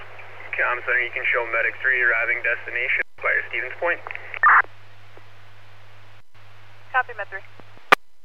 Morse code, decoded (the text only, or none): none